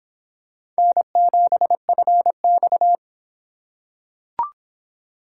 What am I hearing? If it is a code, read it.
Morse code, heard as a keyed tone, about 26 wpm, 700 Hz. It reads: N7FX